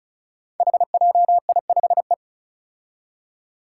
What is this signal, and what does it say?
Morse code, heard as a keyed tone, about 34 wpm, 700 Hz. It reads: HJI5E
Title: Five Challenging Letter-Number Combinations 35wpm